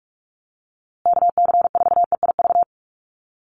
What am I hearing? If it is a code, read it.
Morse code, heard as a keyed tone, about 44 wpm, 690 Hz. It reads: KC4EIV
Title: USA Call Signs Repeated 3x 45wpm